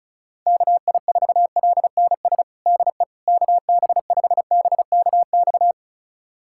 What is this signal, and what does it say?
Morse code, heard as a keyed tone, about 36 wpm, 700 Hz. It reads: KI4LNS DE KB5BKX
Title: USA Callsign DE Callsign 35wpm